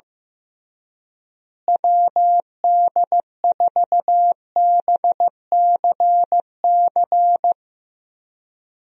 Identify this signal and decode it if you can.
Morse code, heard as a keyed tone, about 15 wpm, 700 Hz. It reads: WD4BCC
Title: USA Call Signs - 6 Characters - Repeated 3x - 15wpm